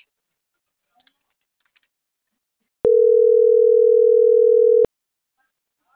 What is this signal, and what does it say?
Ringback tone (North America), 440 + 480 Hz.